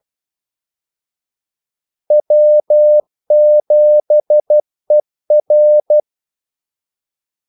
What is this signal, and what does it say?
Morse code, heard as a keyed tone, about 12 wpm, 600 Hz. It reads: W7ER
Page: 2 ekstra ordmellemrum mellem kaldesignalerne